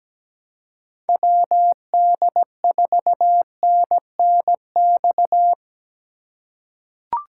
Morse code, heard as a keyed tone, about 17 wpm, 700 Hz. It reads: WD4NNX
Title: USA Call Signs - 6 Characters - 3x Character Spacing - 17wpm